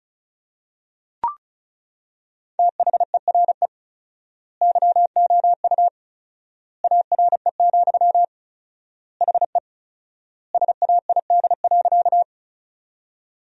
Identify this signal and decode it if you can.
Morse code, heard as a keyed tone, about 36 wpm, 700 Hz. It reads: THERE YOU ARE, HE SAID.
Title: 4x Word-Spacing - Sentences from Top 100 Words - 35wpm